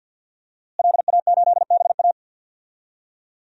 Morse code, heard as a keyed tone, about 50 wpm, 700 Hz. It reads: LA8BA
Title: DXCC-100 Call Signs - 5 Characters - Repeated 2x - 50wpm